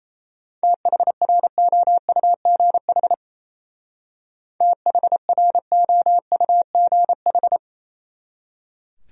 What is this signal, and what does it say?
Morse code, heard as a keyed tone, about 35 wpm, 700 Hz. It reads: THROUGH THROUGH
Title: Speed-Racing Top 200 Words 28wpm